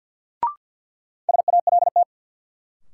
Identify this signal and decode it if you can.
Morse code, heard as a keyed tone, about 53 wpm, 700 Hz. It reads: SALT
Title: Sentences from Top 800 Words - New Word Drill - 50wpm